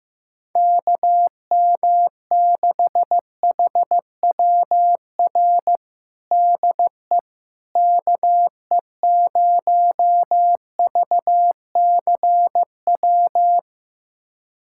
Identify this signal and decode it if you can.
Morse code, heard as a keyed tone, about 15 wpm, 700 Hz. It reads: KM6HWR DE KE0VCW
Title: USA Callsign DE Callsign - 6 Characters - 15wpm